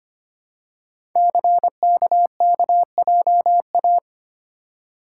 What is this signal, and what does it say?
Morse code, heard as a keyed tone, about 25 wpm, 700 Hz. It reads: CKKJA
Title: Code Groups - 5 Letters - 25wpm